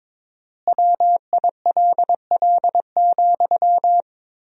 Morse code, heard as a keyed tone, about 22 wpm, 700 Hz. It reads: WILL,
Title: 4x Word-Spacing - Sentences from Top 200 Words - 22wpm